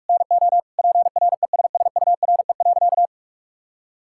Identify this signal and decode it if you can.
Morse code, heard as a keyed tone, about 46 wpm, 700 Hz. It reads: NO PRESSURE.
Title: Sentences from Top 1000 Words - Review Part 6 - 45wpm